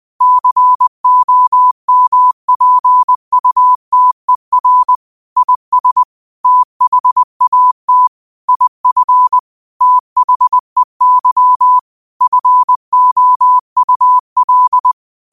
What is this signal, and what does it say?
Morse code, heard as a keyed tone, about 20 wpm, 1000 Hz. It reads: COMPUTER IS THAT IF THEY FOUL